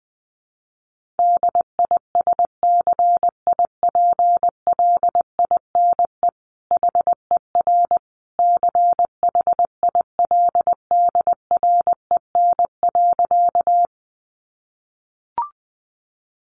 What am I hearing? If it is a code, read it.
Morse code, heard as a keyed tone, about 20 wpm, 700 Hz. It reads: DISCIPLINE HER CHILDREN.
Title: Sentences from Top 900 Words 20wpm